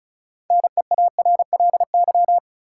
Morse code, heard as a keyed tone, about 36 wpm, 700 Hz. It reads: NEARLY
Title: Sentences from Top 1300 Words - New Word Drill - 35wpm